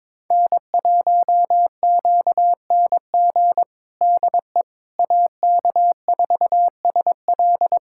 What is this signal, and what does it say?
Morse code, heard as a keyed tone, about 22 wpm, 700 Hz. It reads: N1QNG DE AK4SL
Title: USA Callsign DE Callsign - 5 Characters - 22wpm